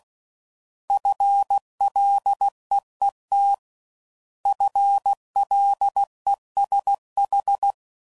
Morse code, heard as a keyed tone, about 16 wpm, 800 Hz. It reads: FLEET FLESH